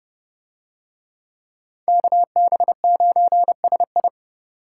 Morse code, heard as a keyed tone, about 29 wpm, 700 Hz. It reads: KB9SI